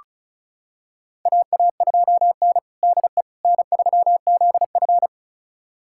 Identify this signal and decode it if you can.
Morse code, heard as a keyed tone, about 36 wpm, 700 Hz. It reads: AA2N DE N3ZF
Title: USA Callsign DE Callsign - 4 Characters - 35wpm